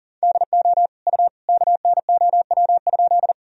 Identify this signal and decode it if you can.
Morse code, heard as a keyed tone, about 37 wpm, 700 Hz. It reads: DO U KNOW?
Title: Sentences from Top 100 Words and Abbreviations 40wpm